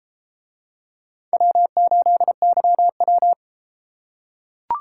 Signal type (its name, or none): Morse code